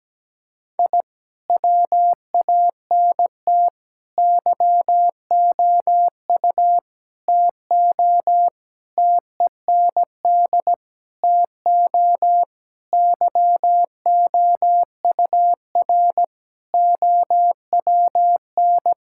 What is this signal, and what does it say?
Morse code, heard as a keyed tone, about 17 wpm, 700 Hz. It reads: I WANT YOU TO TEND TO YOUR OWN
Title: Sentences from Top 1000 Words - Review Part 5 - 17wpm